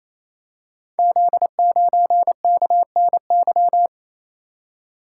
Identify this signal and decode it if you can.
Morse code, heard as a keyed tone, about 27 wpm, 700 Hz. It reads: Z9KNY
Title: Code Groups - 5 Letter:Numbers - 28wpm